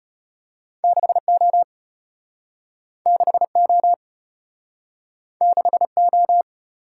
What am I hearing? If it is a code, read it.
Morse code, heard as a keyed tone, about 34 wpm, 700 Hz. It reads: 6O 6O 6O